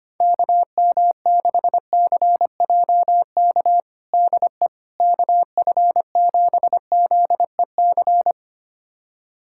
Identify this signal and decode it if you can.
Morse code, heard as a keyed tone, about 25 wpm, 700 Hz. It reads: KM6CJK DE KF7ZEC
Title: USA Callsign DE Callsign - 6 Characters - 25wpm